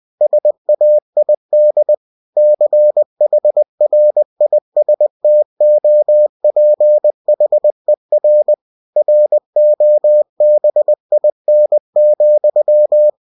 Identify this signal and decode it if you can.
Morse code, heard as a keyed tone, about 20 wpm, 600 Hz. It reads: SAID CHRISTOPHER ROBIN,
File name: cw-books / winnie-the-pooh / winnie-20wpm-page-0096